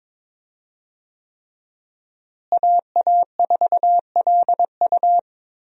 Morse code, heard as a keyed tone, about 22 wpm, 700 Hz. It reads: AA4LU
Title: USA Call Signs - 5 Characters - Repeated 3x - 22wpm